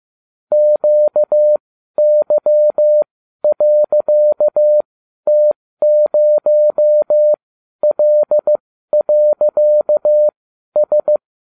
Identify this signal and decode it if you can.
Morse code, heard as a keyed tone, about 15 wpm, spaced as for 12 wpm, 600 Hz. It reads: QY.T0L.S
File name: cw / ubacw30 3